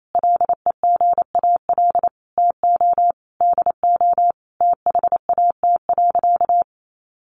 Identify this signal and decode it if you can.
Morse code, heard as a keyed tone, about 28 wpm, 700 Hz. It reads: LEGAL TO DO THAT.
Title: Sentences from Top 1200 Words 28wpm